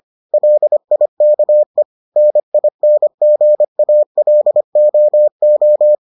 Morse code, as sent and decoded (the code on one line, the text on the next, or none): .-.. .. -.- . / -. .. -. --. .- .-.. --- ---
LIKE NINGALOO